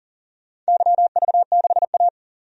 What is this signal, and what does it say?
Morse code, heard as a keyed tone, about 40 wpm, 700 Hz. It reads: YV6A
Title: DXCC-100 Call Signs - 4 Characters - 40wpm